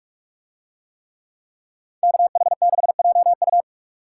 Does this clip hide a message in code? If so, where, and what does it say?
Yes: Morse code from 2.0 to 3.6 s, heard as a keyed tone, about 44 wpm, 700 Hz. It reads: KH6JU